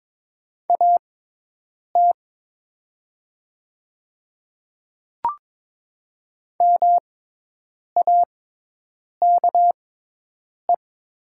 Morse code, heard as a keyed tone, about 22 wpm, spaced as for 8 wpm, 700 Hz. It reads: AT MAKE
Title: 6x Character Spacing - Top 100 Words - 22wpm